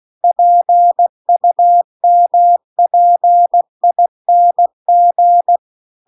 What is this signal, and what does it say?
Morse code, heard as a keyed tone, about 16 wpm, 700 Hz. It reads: PUMPING
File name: CW-The-Big-Twoheaded-River-2-16WPM